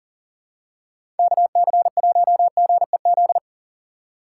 Morse code, heard as a keyed tone, about 40 wpm, 700 Hz. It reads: KC1GEZ
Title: USA Call Signs - 6 Characters - 3x Character Spacing - 40wpm